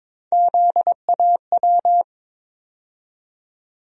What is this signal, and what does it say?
Morse code, heard as a keyed tone, about 21 wpm, 700 Hz. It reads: ZAW
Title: USA Call Sign Postfixes - 3 Letters 22wpm